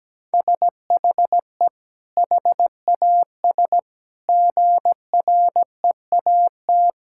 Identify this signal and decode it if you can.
Morse code, heard as a keyed tone, about 17 wpm, 700 Hz. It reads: SHE HAS GREAT
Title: Sentences from Top 1000 Words - Review Part 6 - 17wpm